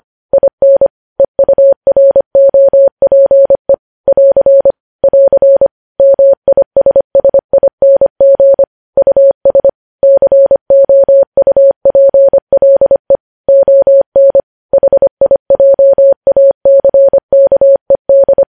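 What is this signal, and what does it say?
Morse code, heard as a keyed tone, about 25 wpm, 550 Hz. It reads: IN EUROPE <AR> <AR> MISSING US COUPLE ON HIJACKED
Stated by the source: Morse code news headlines at 25 wpm.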